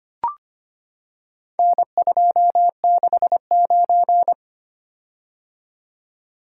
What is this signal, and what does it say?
Morse code, heard as a keyed tone, about 25 wpm, 700 Hz. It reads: N269
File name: practice / 25wpm / Four Letter-Number Combinations - 25wpm